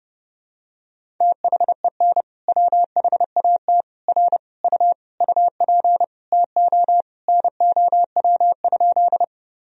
Morse code, heard as a keyed tone, about 30 wpm, 700 Hz. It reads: THEN WHAT R U UP TO NOW?
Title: Sentences from Top 100 Words and Abbreviations 30wpm